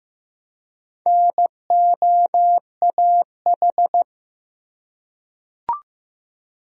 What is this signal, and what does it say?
Morse code, heard as a keyed tone, about 15 wpm, 700 Hz. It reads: NOAH
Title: USA Common Names - Max 4 Letters - 15wpm